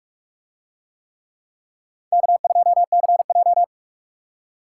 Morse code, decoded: K2CJ

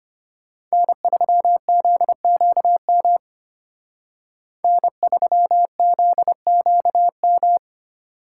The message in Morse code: N3ZQM N3ZQM

